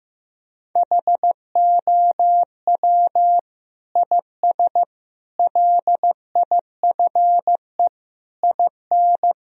Morse code, heard as a keyed tone, about 16 wpm, 700 Hz. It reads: HOW IS LIFE IN
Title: Sentences from Top 1000 Words - Review Part 2 - 15wpm